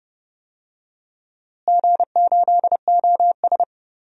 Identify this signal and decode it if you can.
Morse code, heard as a keyed tone, about 30 wpm, 700 Hz. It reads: G8OS
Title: DXCC-100 Call Signs - 4 Characters - Repeated 2x - 30wpm